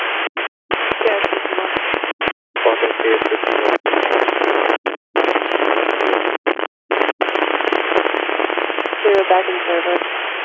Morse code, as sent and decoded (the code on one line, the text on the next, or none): none